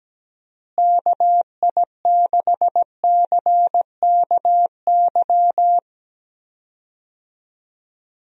Morse code, heard as a keyed tone, about 17 wpm, 700 Hz. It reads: KI6CKY
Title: USA Call Signs - 6 Characters - Repeated 3x - 17wpm